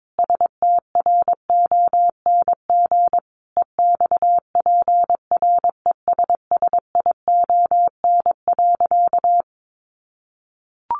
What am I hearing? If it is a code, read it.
Morse code, heard as a keyed tone, about 22 wpm, 700 Hz. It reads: STRONG EXPRESSION.
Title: Sentences from Top 1200 Words 22wpm